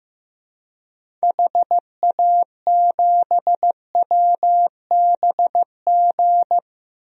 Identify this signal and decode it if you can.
Morse code, heard as a keyed tone, about 15 wpm, 700 Hz. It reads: HA7WBG
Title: DXCC-100 Call Signs - 6 Characters - Repeated 2x - 15wpm